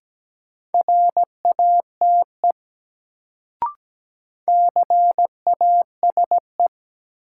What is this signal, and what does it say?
Morse code, heard as a keyed tone, about 17 wpm, 700 Hz. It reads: RATE CASE
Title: Sentences from Top 700 Words - New Word Drill - 17wpm